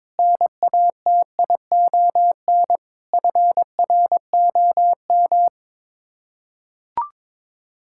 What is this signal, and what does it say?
Morse code, heard as a keyed tone, about 22 wpm, 700 Hz. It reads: NATION FROM
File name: practice / 22wpm / Build A Phrase - 5-Word Phrases - 22wpm